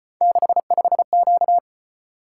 Morse code, heard as a keyed tone, about 34 wpm, 700 Hz. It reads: 65Q